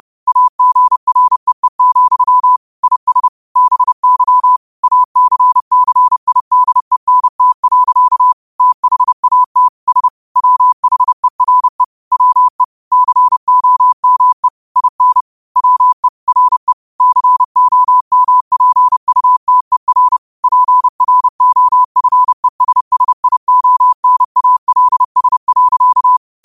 Morse code, heard as a keyed tone, about 30 wpm, 1000 Hz. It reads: AGREE, IS BY ACCIDENT. THAT S WHERE WE COME IN WE RE COMPUTER PROFESSIONALS.
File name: qotdfiles / 30wpm